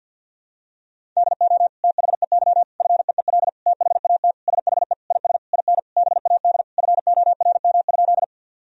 Morse code, heard as a keyed tone, about 50 wpm, 700 Hz. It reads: DO THEY FEEL THAT SHE IS IN BAD FORM?
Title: Sentences from Top 1000 Words - Review Part 1 - 50wpm